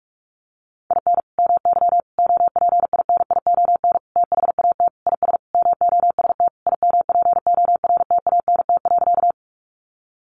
Morse code, heard as a keyed tone, about 44 wpm, 700 Hz. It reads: IN MY OPINION THAT IS MOST IMPORTANT.
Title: Sentences from Top 1200 Words 45wpm